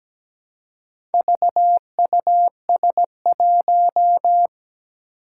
Morse code, heard as a keyed tone, about 17 wpm, 700 Hz. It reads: VUS1
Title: Four Challenging Letter-Number Combinations 17wpm